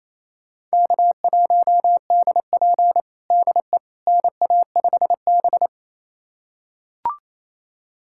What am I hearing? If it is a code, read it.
Morse code, heard as a keyed tone, about 28 wpm, 700 Hz. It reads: K1DP DE NA5B
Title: USA Callsign DE Callsign - 4 Characters - 28wpm